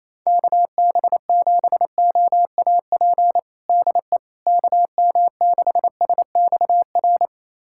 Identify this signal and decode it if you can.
Morse code, heard as a keyed tone, about 28 wpm, 700 Hz. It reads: KB7OAP DE KM6SXR
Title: USA Callsign DE Callsign - 6 Characters - 28wpm